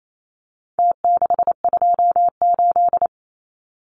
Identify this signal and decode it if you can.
Morse code, heard as a keyed tone, about 28 wpm, 700 Hz. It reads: T628
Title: Numbers - Contest Style - 4-Digit 28wpm